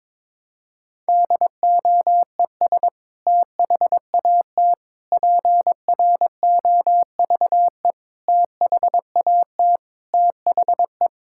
Morse code, heard as a keyed tone, about 22 wpm, 700 Hz. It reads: DOES THAT PROVE THAT THE
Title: Sentences from Top 1600 Words 22wpm